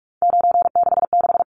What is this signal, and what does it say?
Morse code, heard as a keyed tone, about 44 wpm, 700 Hz. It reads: 966
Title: Numbers - 3-Digit 45wpm